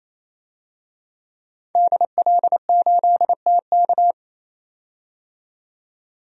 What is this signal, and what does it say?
Morse code, heard as a keyed tone, about 28 wpm, 700 Hz. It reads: DL8TK
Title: DXCC-100 Call Signs - 5 Characters - 28wpm